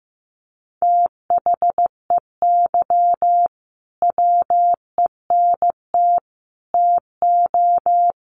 Morse code, heard as a keyed tone, about 15 wpm, 700 Hz. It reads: THEY WENT TO
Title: Sentences from Top 900 Words 15wpm